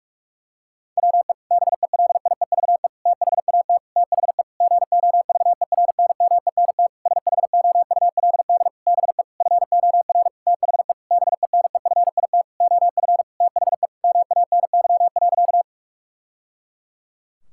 Morse code, heard as a keyed tone, about 46 wpm, 700 Hz. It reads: WE BELIEVE THAT THE GOVERNMENT SHOULD BE FOR THE BENEFIT OF THE MANY.
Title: Sentences from Top 1000 Words - Review Part 6 - 45wpm